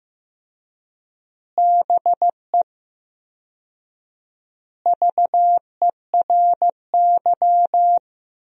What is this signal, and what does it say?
Morse code, heard as a keyed tone, about 16 wpm, 700 Hz. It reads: BE VERY
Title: 4x Word-Spacing - Sentences from Top 200 Words - 15wpm